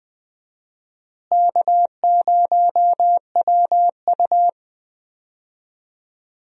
Morse code, heard as a keyed tone, about 20 wpm, 700 Hz. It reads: K0WU
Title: USA Call Signs - 4 Characters - Repeated 2x - 20wpm